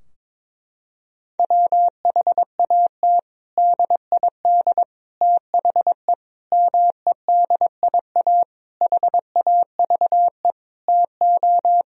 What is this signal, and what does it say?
Morse code, heard as a keyed tone, about 21 wpm, 700 Hz. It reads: WHAT DID THE MEDIA HAVE TO
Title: Sentences from Top 1000 Words - Review Part 4 - 22wpm